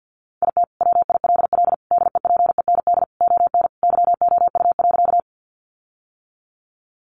Morse code, heard as a keyed tone, about 50 wpm, 700 Hz. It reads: IT WILL DEPEND ON YOU.
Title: Sentences from Top 1300 Words 50wpm